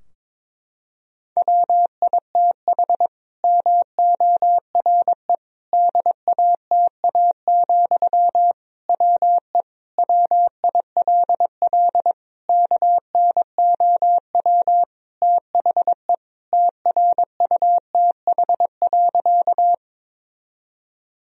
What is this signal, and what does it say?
Morse code, heard as a keyed tone, about 22 wpm, 700 Hz. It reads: WITH MORE DATA, WE WILL KNOW THE TRUTH.
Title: Sentences from Top 1000 Words - Review Part 3 - 22wpm